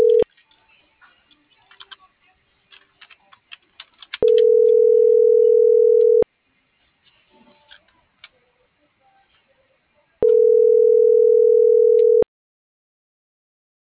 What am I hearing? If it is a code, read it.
Ringback tone (North America), 440 + 480 Hz.